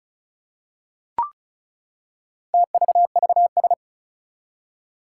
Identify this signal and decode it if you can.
Morse code, heard as a keyed tone, about 36 wpm, 700 Hz. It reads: TVVS
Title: Four Challenging Letter-Number Combinations 35wpm